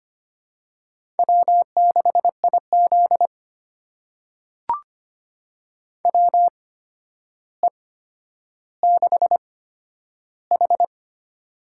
Morse code, heard as a keyed tone, about 25 wpm, spaced as for 11 wpm, 700 Hz. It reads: W6IZ WE6H
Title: USA Call Signs - 4 Characters - 8x Character Spacing - 25wpm